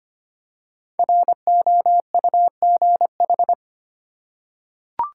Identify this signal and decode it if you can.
Morse code, heard as a keyed tone, about 25 wpm, 700 Hz. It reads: ROUGH
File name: 5 Letter Words - First 3 Letters Given - 25wpm